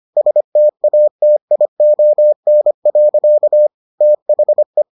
Morse code, heard as a keyed tone, about 25 wpm, 600 Hz. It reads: STATION. THE